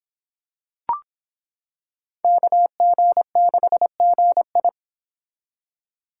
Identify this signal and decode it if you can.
Morse code, heard as a keyed tone, about 25 wpm, 700 Hz. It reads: KG6GI